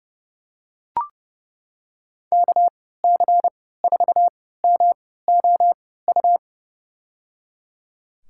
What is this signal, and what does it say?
Morse code, heard as a keyed tone, about 31 wpm, spaced as for 17 wpm, 700 Hz. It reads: KC4MOU
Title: USA Call Signs - 3x Character Spacing - 30wpm